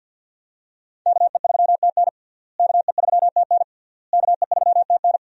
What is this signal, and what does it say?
Morse code, heard as a keyed tone, about 53 wpm, 700 Hz. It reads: KE3TN KE3TN KE3TN
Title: USA Call Signs - 5 Characters - Repeated 3x - 50wpm